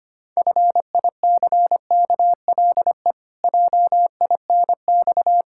Morse code, heard as a keyed tone, about 25 wpm, 700 Hz. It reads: FICKLE JINX
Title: Pangram Challenge v2 - 25wpm